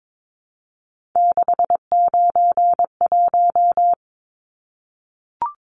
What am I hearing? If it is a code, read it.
Morse code, heard as a keyed tone, about 22 wpm, 700 Hz. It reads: 691
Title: Numbers - 3-Digit 22wpm